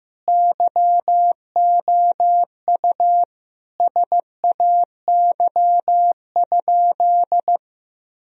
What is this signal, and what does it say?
Morse code, heard as a keyed tone, about 15 wpm, 700 Hz. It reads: YOU SAY?
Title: Sentences from Top 1000 Words - Review Part 1 - 15wpm